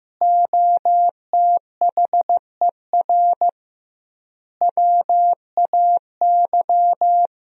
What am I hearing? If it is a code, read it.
Morse code, heard as a keyed tone, about 15 wpm, 700 Hz. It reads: OTHER WAY
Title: 2x Word-Spacing - Sentences from Top 200 Words - 15wpm